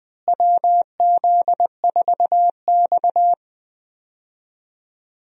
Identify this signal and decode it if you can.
Morse code, heard as a keyed tone, about 20 wpm, 700 Hz. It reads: WZ4X
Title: USA Call Signs - 4 Characters - Repeated 3x - 20wpm